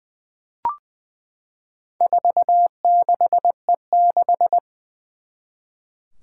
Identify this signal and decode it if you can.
Morse code, heard as a keyed tone, about 20 wpm, 700 Hz. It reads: 46E6
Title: Four Challenging Letter-Number Combinations 20wpm